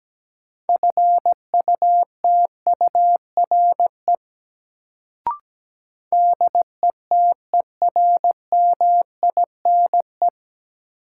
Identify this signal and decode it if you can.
Morse code, heard as a keyed tone, about 17 wpm, 700 Hz. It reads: FUTURE DETERMINE
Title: Sentences from Top 700 Words - New Word Drill - 17wpm